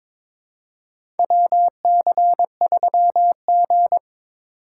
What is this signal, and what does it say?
Morse code, heard as a keyed tone, about 22 wpm, 700 Hz. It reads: WC3G
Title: USA Call Signs - 4 Characters - 3x Character Spacing - 22wpm